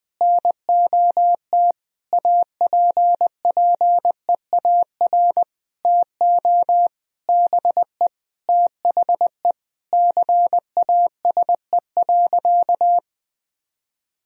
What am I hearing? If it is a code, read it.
Morse code, heard as a keyed tone, about 20 wpm, 700 Hz. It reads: NOT APPEAR TO BE THE CASE.
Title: Sentences from Top 1000 Words - Review Part 2 - 20wpm